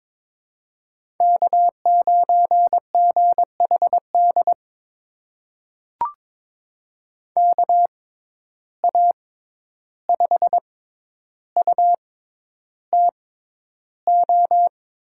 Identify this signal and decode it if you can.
Morse code, heard as a keyed tone, about 22 wpm, spaced as for 11 wpm, 700 Hz. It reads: K9GHD KA5UTO
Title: USA Call Signs - 6x Character Spacing - 22wpm